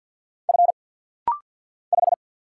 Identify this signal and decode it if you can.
Morse code, heard as a keyed tone, about 48 wpm, 690 Hz. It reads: F 5
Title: Single Letter-Number Alt - Rapid-Fire 50wpm